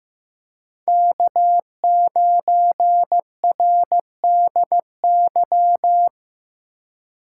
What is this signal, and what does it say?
Morse code, heard as a keyed tone, about 16 wpm, 700 Hz. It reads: K9RDY
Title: USA Call Signs - 5 Characters - 4x Character Spacing - 15wpm